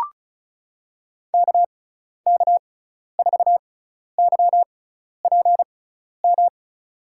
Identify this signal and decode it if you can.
Morse code, heard as a keyed tone, about 36 wpm, spaced as for 12 wpm, 690 Hz. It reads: KK4YPM
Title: USA Call Signs - 6 Characters - 6x Character Spacing - 35wpm